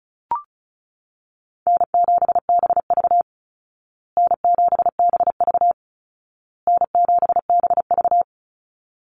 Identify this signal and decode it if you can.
Morse code, heard as a keyed tone, about 34 wpm, 700 Hz. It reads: N7BV N7BV N7BV
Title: Easy Call Signs Repeated 3x - 35wpm